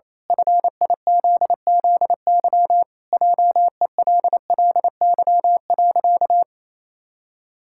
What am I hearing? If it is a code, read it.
Morse code, heard as a keyed tone, about 28 wpm, 700 Hz. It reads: FIZZY JELLY.
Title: Pangram Challenge v2 - 28wpm